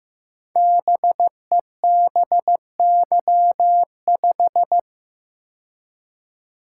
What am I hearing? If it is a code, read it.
Morse code, heard as a keyed tone, about 15 wpm, 700 Hz. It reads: BEBY5
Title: Code Groups - 5 Letter:Numbers - 15wpm